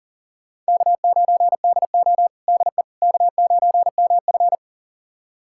Morse code, heard as a keyed tone, about 42 wpm, 690 Hz. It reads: K9DO DE K9MF
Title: USA Callsign DE Callsign - 4 Characters - 40wpm